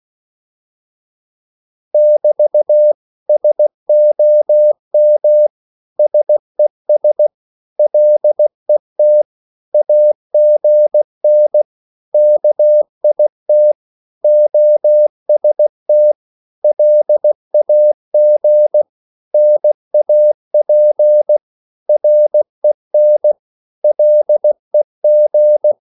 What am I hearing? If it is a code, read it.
Morse code, heard as a keyed tone, about 16 wpm, 600 Hz. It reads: <BT> SOM SES LET AGN KIT OST LAG NAP REN LEG
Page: Korte danske ord 16wpm